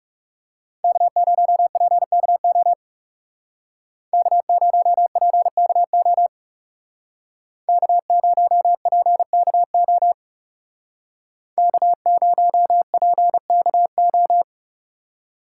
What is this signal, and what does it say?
Morse code, heard as a keyed tone, about 35 wpm, 700 Hz. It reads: K0PKO K0PKO K0PKO K0PKO
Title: Speed Racing - USA Call Signs 30wpm